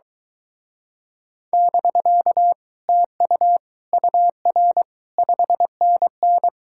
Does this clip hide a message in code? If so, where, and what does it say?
Yes: Morse code from 1.5 to 6.5 s, heard as a keyed tone, about 22 wpm, 700 Hz. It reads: <BK> TU UR 5NN